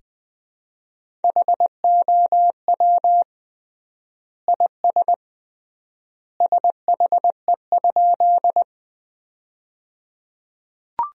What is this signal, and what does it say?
Morse code, heard as a keyed tone, about 20 wpm, 700 Hz. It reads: HOW IS SHE?
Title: 3x Word-Spacing - Sentences from Top 100 Words - 20wpm